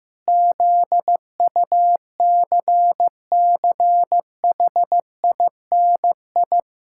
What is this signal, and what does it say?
Morse code, heard as a keyed tone, about 15 wpm, 700 Hz. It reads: ZUCCHINI
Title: Pangram Challenge v2 - 15wpm